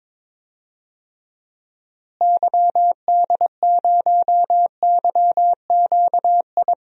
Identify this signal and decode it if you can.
Morse code, heard as a keyed tone, about 22 wpm, 700 Hz. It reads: YD0YQI
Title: DXCC-100 Call Signs - 6 Characters - Repeated 2x - 22wpm